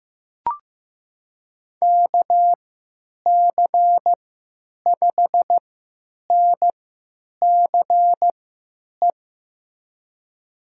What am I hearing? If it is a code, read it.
Morse code, heard as a keyed tone, about 15 wpm, spaced as for 9 wpm, 700 Hz. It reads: KC5NCE
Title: USA Call Signs - 6 Characters - 3x Character Spacing - 15wpm